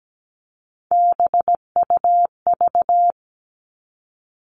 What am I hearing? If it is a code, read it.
Morse code, heard as a keyed tone, about 17 wpm, 700 Hz. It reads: BUV
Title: All Cut Numbers - 3-Digit 17wpm